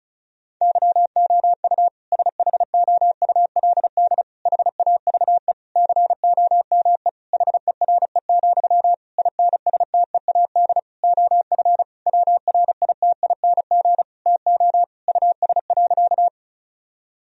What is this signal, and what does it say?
Morse code, heard as a keyed tone, about 36 wpm, 700 Hz. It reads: YOU SHOULD HAVE COME HERE, INSTEAD OF WRITING TO US.
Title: Sentences from Top 1000 Words - Review Part 4 - 35wpm